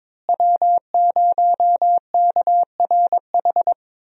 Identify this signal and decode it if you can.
Morse code, heard as a keyed tone, about 22 wpm, 700 Hz. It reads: W0KRH
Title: USA Call Signs - 5 Characters - Repeated 2x - 22wpm